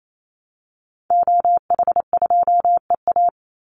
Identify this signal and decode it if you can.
Morse code, heard as a keyed tone, about 28 wpm, 700 Hz. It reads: OH2EA
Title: Call Signs Repeated 2x - 28wpm